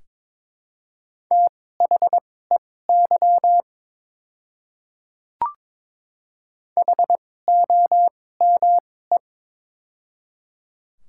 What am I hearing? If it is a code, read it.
Morse code, heard as a keyed tone, about 23 wpm, spaced as for 16 wpm, 700 Hz. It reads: THEY HOME
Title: 2x Character Spacing - Top 200 Words - 22wpm